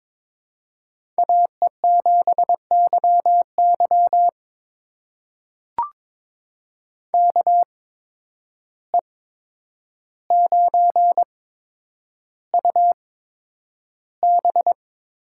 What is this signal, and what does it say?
Morse code, heard as a keyed tone, about 22 wpm, spaced as for 9 wpm, 700 Hz. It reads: AE7YY KE9UB
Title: USA Call Signs - 5 Characters - 8x Character Spacing - 22wpm